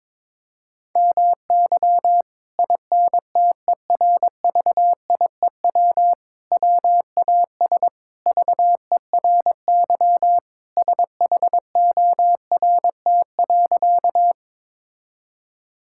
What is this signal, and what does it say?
Morse code, heard as a keyed tone, about 22 wpm, 700 Hz. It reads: MY INTERVIEW WAS VERY SHORT.
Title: Sentences from Top 1600 Words 22wpm